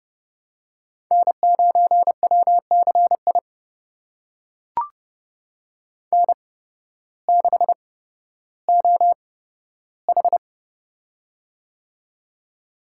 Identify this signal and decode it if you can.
Morse code, heard as a keyed tone, about 30 wpm, spaced as for 14 wpm, 700 Hz. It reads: N9WCI N6OH
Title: USA Call Signs - 8x Character Spacing - 30wpm